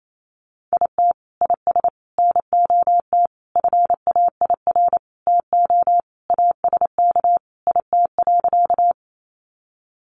Morse code, heard as a keyed tone, about 28 wpm, 700 Hz. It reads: IT IS NOT FAIR TO ASK IT.
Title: Sentences from Top 1300 Words 28wpm